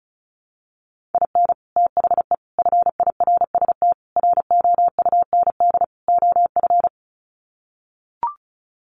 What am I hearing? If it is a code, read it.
Morse code, heard as a keyed tone, about 34 wpm, 700 Hz. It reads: IN THE FIRST ROUND OF
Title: Sets of 5 Words Encore 35wpm